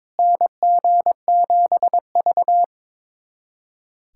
Morse code, heard as a keyed tone, about 22 wpm, 700 Hz. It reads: NG7V
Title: USA Callsign DE Callsign - 4 Characters - 22wpm